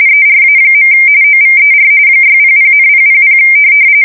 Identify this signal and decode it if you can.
RTTY, about 45.45 baud. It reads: ERA. OVER 600,000
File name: fsk170 / teletype-corp